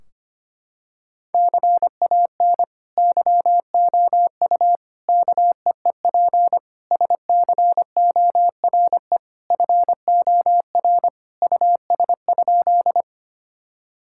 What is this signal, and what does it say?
Morse code, heard as a keyed tone, about 25 wpm, 700 Hz. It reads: CAN YOU KEEP SCORE FOR US?
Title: Sentences from Top 1600 Words 25wpm